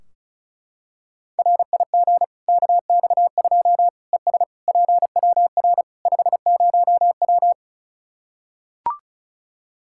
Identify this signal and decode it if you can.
Morse code, heard as a keyed tone, about 36 wpm, 700 Hz. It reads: RIG KX2 ES PWR 50W
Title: Practice QSO - Exchange 2 - Info - 35wpm